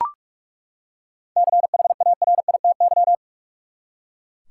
Morse code, heard as a keyed tone, about 42 wpm, 700 Hz. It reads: CHARITY
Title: Seven Letter Words - 45wpm